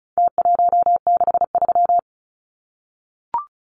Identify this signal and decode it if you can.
Morse code, heard as a keyed tone, about 34 wpm, 700 Hz. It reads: T163
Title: Numbers - Contest Style - 4-Digit 35wpm